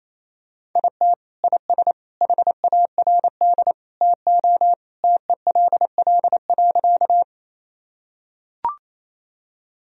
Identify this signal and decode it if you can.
Morse code, heard as a keyed tone, about 28 wpm, 700 Hz. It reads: IT IS HARD TO TELL.
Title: Sentences from Top 1000 Words - Review Part 1 - 28wpm